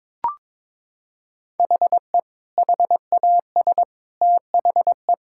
Morse code, heard as a keyed tone, about 22 wpm, 700 Hz. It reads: HE HAS THE
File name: practice / 22wpm / Sentences with 3-Word Phrases - 22wpm